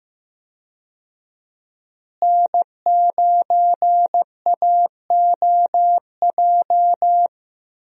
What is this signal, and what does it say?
Morse code, heard as a keyed tone, about 15 wpm, 700 Hz. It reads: N9AOJ
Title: USA Call Signs - 5 Characters - Repeated 3x - 15wpm